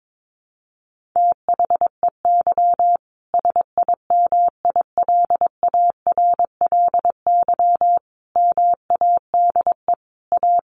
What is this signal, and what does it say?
Morse code, heard as a keyed tone, about 22 wpm, 700 Hz. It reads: THEY SIMILARLY MADE A
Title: Sentences from Top 1400 Words 22wpm